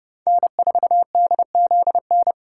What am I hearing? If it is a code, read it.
Morse code, heard as a keyed tone, about 28 wpm, 700 Hz. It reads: N4DZN